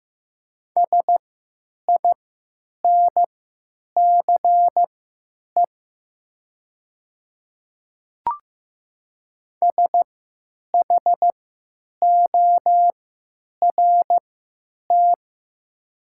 Morse code, heard as a keyed tone, about 15 wpm, spaced as for 9 wpm, 700 Hz. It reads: SINCE SHORT
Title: 3x Character Spacing - Top 500 Words - 15wpm